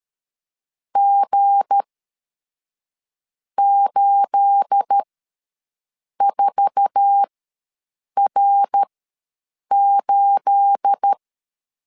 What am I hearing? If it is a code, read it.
Morse code, heard as a keyed tone, about 13 wpm, spaced as for 6 wpm, 800 Hz. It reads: G84R8
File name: LEZIONI / 9 28 ws 12wpm 3354